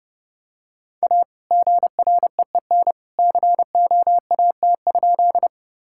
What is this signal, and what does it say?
Morse code, heard as a keyed tone, about 30 wpm, 700 Hz. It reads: A GREEN COAT?
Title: Sentences from Top 1600 Words 30wpm